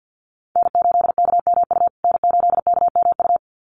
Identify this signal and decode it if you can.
Morse code, heard as a keyed tone, about 50 wpm, 700 Hz. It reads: N8KMU N8KMU
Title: USA Call Signs Repeated 2x 50wpm